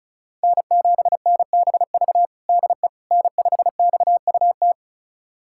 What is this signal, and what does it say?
Morse code, heard as a keyed tone, about 36 wpm, 700 Hz. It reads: N7NBV DE N5XUT
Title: USA Callsign DE Callsign - 5 Characters - 35wpm